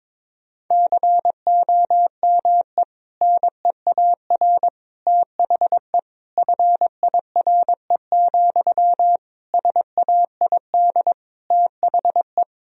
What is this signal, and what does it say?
Morse code, heard as a keyed tone, about 21 wpm, 700 Hz. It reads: COME NEAR THE FIRE, SAID THE
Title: Sentences from Top 1000 Words - Review Part 1 - 22wpm